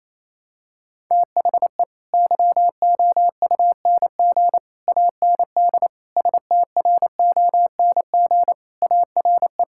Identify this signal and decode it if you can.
Morse code, heard as a keyed tone, about 28 wpm, 700 Hz. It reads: THE YOUNG AND STRONG ARE
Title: Sentences from Top 1000 Words - Review Part 5 - 28wpm